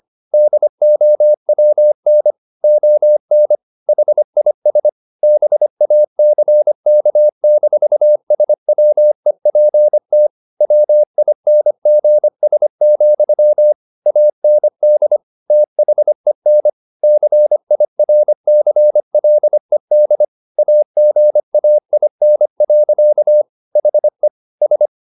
Morse code, heard as a keyed tone, about 25 wpm, 600 Hz. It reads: DOWN ON HIS BACK-SWEPT WINGS, AND THEN CIRCLED AGAIN. HE S
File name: the old man and the sea part 1 25wpm